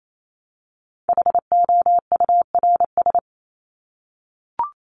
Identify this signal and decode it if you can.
Morse code, heard as a keyed tone, about 28 wpm, 700 Hz. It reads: HOURS
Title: ES:S Suffix - Top 100 Words with Base Words 28wpm